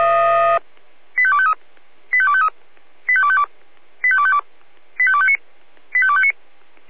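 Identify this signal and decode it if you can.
Five-tone selective call (ZVEI), heard as quick runs of five tones, tones of about 70 ms. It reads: 85253 85253 85252 85252 85259 85259
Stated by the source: ZVEI five-tone call received on SDR.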